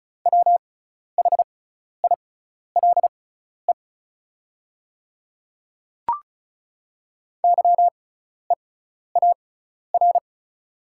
Morse code, heard as a keyed tone, about 36 wpm, spaced as for 12 wpm, 690 Hz. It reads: WHILE YEAR
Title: 6x Character Spacing - Top 300 Words - 35wpm